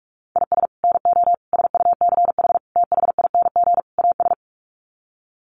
Morse code, heard as a keyed tone, about 44 wpm, 700 Hz. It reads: IS NO SUCH THING AS
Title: Sets of 5 Words Encore 45wpm